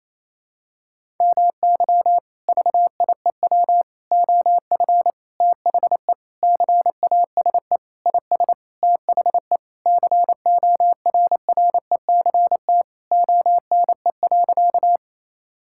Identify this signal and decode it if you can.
Morse code, heard as a keyed tone, about 28 wpm, 700 Hz. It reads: MY VIEW OF THE CASE IS THE CORRECT ONE.
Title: Sentences from Top 1000 Words - Review Part 6 - 28wpm